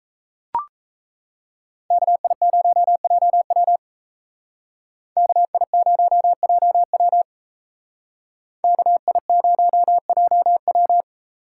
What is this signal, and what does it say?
Morse code, heard as a keyed tone, about 36 wpm, 700 Hz. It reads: KI0JW KI0JW KI0JW